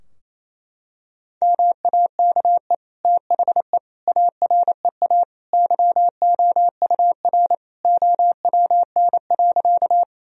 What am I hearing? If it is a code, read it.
Morse code, heard as a keyed tone, about 28 wpm, 700 Hz. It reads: MAKE THE AREA YOUR OWN.
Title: Sentences from Top 1000 Words - Review Part 2 - 28wpm